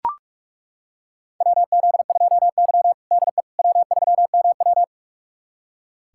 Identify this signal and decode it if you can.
Morse code, heard as a keyed tone, about 41 wpm, 700 Hz. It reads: WZ2Y DE W3MW
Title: USA Callsign DE Callsign - 4 Characters - 45wpm